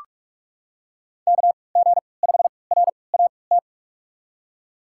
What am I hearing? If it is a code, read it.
Morse code, heard as a keyed tone, about 42 wpm, spaced as for 25 wpm, 700 Hz. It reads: KG5RAT
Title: USA Call Signs - 6 Characters - 3x Character Spacing - 45wpm